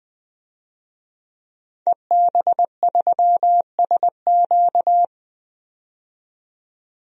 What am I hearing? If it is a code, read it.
Morse code, heard as a keyed tone, about 20 wpm, 700 Hz. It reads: EB3SQ
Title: DXCC-100 Call Signs - 5 Characters - 20wpm